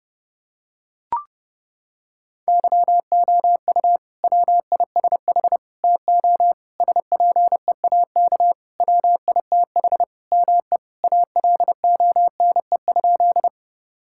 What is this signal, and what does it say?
Morse code, heard as a keyed tone, about 30 wpm, 700 Hz. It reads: YOU WISH TO SPEAK WITH ME ALONE?
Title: Sentences from Top 1600 Words 30wpm